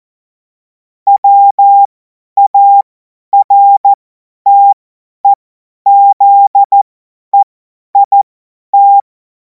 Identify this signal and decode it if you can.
Morse code, heard as a keyed tone, about 14 wpm, spaced as for 10 wpm, 800 Hz. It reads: WARTEZEIT